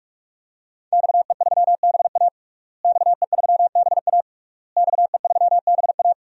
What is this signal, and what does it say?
Morse code, heard as a keyed tone, about 46 wpm, 700 Hz. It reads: XE3BA XE3BA XE3BA
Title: DXCC-100 Call Signs - 5 Characters - Repeated 3x - 45wpm